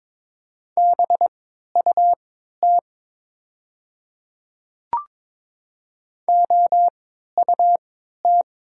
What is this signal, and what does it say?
Morse code, heard as a keyed tone, about 22 wpm, spaced as for 13 wpm, 700 Hz. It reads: BUT OUT
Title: 3x Character Spacing - Top 500 Words - 22wpm